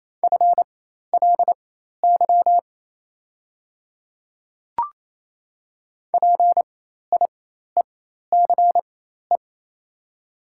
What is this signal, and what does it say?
Morse code, heard as a keyed tone, about 27 wpm, spaced as for 13 wpm, 690 Hz. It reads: FLY PIECE
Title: 4x Character Spacing - Top 500 Words - 28wpm